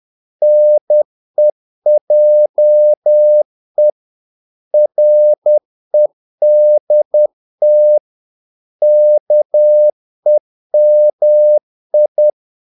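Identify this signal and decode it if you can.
Morse code, heard as a keyed tone, about 10 wpm, 600 Hz. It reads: NEJE REDT KEMI
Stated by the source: Morse practice of short Danish words at 10 wpm, from CW med Gnister.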